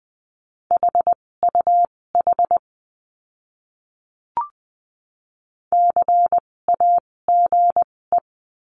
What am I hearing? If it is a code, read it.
Morse code, heard as a keyed tone, about 20 wpm, spaced as for 16 wpm, 700 Hz. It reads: HUH CAGE
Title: 256 Lesson 42a - Words 12-20wpm